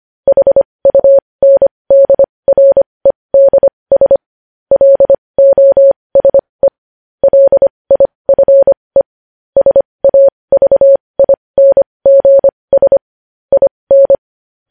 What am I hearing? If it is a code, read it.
Morse code, heard as a keyed tone, about 25 wpm, spaced as for 20 wpm, 550 Hz. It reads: HUNDREDS LOSE LIFE SAVINGS IN
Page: news-headlines-in-morse-code-at-20-wpm